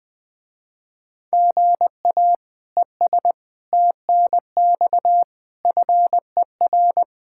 Morse code, heard as a keyed tone, about 21 wpm, 700 Hz. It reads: GA ES TNX FER
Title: Speed-Racing Common QSO Phrases Encore - 20wpm